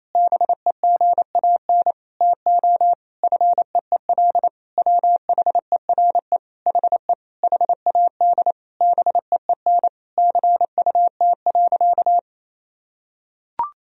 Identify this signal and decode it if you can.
Morse code, heard as a keyed tone, about 28 wpm, 700 Hz. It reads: BEGAN TO FEEL WHERE HE HAD BEEN CUT.
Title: Sentences from Top 1000 Words - Review Part 1 - 28wpm